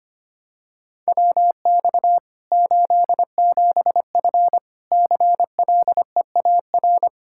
Morse code, heard as a keyed tone, about 25 wpm, 700 Hz. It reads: WX 87F CLEAR
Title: Practice QSO - Exchange 2 - Info - 25wpm